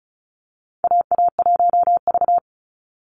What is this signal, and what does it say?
Morse code, heard as a keyed tone, about 34 wpm, 700 Hz. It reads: AA1V
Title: Call Signs Repeated 2x - 35wpm